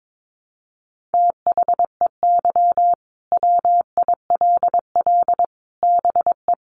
Morse code, heard as a keyed tone, about 22 wpm, 700 Hz. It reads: THEY WILL BE
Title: Sentences from Top 600 Words 22wpm